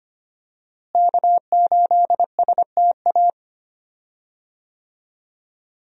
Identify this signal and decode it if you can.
Morse code, heard as a keyed tone, about 25 wpm, 700 Hz. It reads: K8STA
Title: USA Call Signs - 5 Characters - Repeated 2x - 25wpm